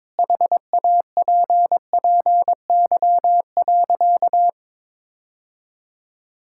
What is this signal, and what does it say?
Morse code, heard as a keyed tone, about 22 wpm, 700 Hz. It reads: HAPPY.
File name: practice / 22wpm / Sentences with 8-Word Phrases - 22wpm